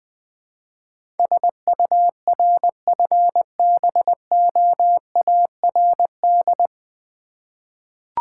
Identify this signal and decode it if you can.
Morse code, heard as a keyed tone, about 20 wpm, 700 Hz. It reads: SURFBOARD
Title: Difficult Words with Hints - 20wpm